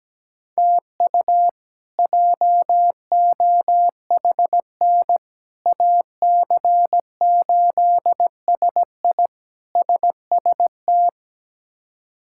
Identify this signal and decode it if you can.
Morse code, heard as a keyed tone, about 17 wpm, 700 Hz. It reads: TU JOHN AC8SI SST
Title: K1USN SST QSO Practice - 17wpm